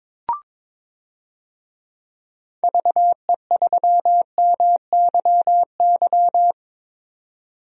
Morse code, heard as a keyed tone, about 22 wpm, 700 Hz. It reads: VE3MYY
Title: DXCC-100 Call Signs - 6 Characters - 22wpm